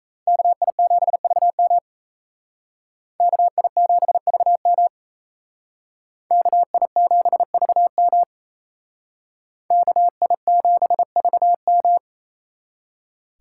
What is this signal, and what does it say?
Morse code, heard as a keyed tone, about 35 wpm, 700 Hz. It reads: KI7VM KI7VM KI7VM KI7VM